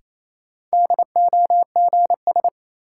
Morse code, heard as a keyed tone, about 28 wpm, 700 Hz. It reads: DOGS
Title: Sentences from Top 1600 Words - New Word Drill Alt - 28wpm